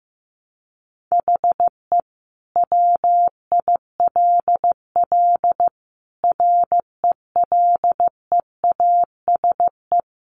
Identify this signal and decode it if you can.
Morse code, heard as a keyed tone, about 15 wpm, 700 Hz. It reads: HE WILL RELEASE
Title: Sentences from Top 1100 Words 15wpm